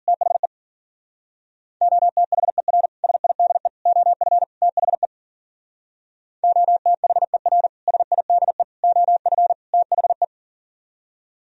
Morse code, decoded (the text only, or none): THE OTHER SIDE OF THE OTHER SIDE OF THE